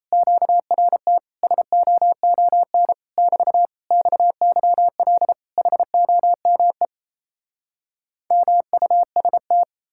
Morse code, heard as a keyed tone, about 37 wpm, 700 Hz. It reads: QRT SOON <BT> XYL HOME MUST